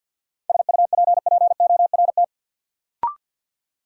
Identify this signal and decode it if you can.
Morse code, heard as a keyed tone, about 53 wpm, 700 Hz. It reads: SUPPORT
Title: Sentences from Top 800 Words - New Word Drill - 50wpm